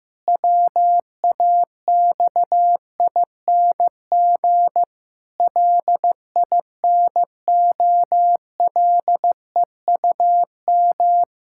Morse code, heard as a keyed tone, about 14 wpm, 700 Hz. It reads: WAXING LINOLEUM
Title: Pangram Challenge v2 - 15wpm